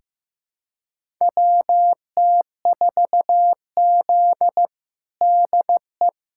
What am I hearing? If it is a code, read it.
Morse code, heard as a keyed tone, about 15 wpm, 700 Hz. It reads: WT4Z DE
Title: USA Callsign DE Callsign - 4 Characters - 15wpm